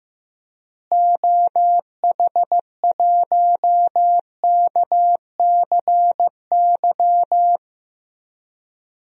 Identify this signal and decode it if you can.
Morse code, heard as a keyed tone, about 15 wpm, 700 Hz. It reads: OH1KCY
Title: DXCC-100 Call Signs - 6 Characters - Repeated 3x - 15wpm